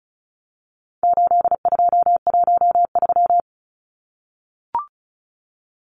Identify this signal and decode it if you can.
Morse code, heard as a keyed tone, about 34 wpm, 700 Hz. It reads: 8213
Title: Numbers - 4-Digit 35wpm